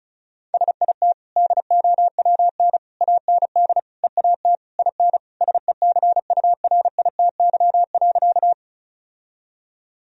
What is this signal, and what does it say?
Morse code, heard as a keyed tone, about 36 wpm, 700 Hz. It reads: SIT DOWN AND EAT IN SECURITY.
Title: Sentences from Top 1000 Words - Review Part 4 - 35wpm